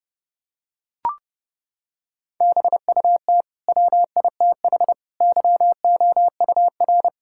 Morse code, heard as a keyed tone, about 30 wpm, 700 Hz. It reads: BUT WITH YOUR
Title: Sentences from Top 1000 Words - Review Part 6 - 30wpm